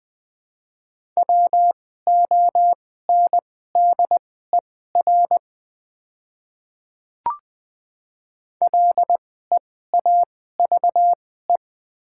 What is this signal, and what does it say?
Morse code, heard as a keyed tone, about 20 wpm, spaced as for 14 wpm, 700 Hz. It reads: WONDER LEAVE
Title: 2x Character Spacing - Top 500 Words - 20wpm